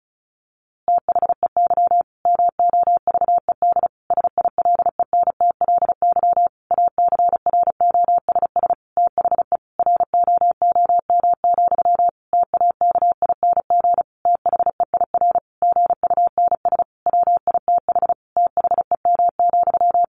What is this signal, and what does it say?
Morse code, heard as a keyed tone, about 34 wpm, 700 Hz. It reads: THEY MOVED SILENTLY ACROSS THE ROOM, TAKING THEIR GUNS WITH THEM,
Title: A Sound Of Thunder 35wpm